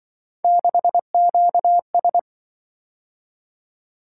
Morse code, heard as a keyed tone, about 24 wpm, 700 Hz. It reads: 6QS
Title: Speed-Racing - Three Letter-Number Combinations - 20wpm